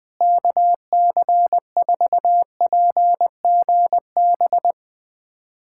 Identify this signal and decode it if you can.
Morse code, heard as a keyed tone, about 20 wpm, 700 Hz. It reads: KC4PGB
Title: USA Callsign DE Callsign 20wpm